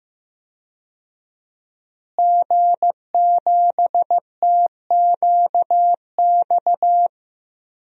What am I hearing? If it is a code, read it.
Morse code, heard as a keyed tone, about 15 wpm, 700 Hz. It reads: G7TQX
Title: DXCC-100 Call Signs - 5 Characters - Repeated 3x - 15wpm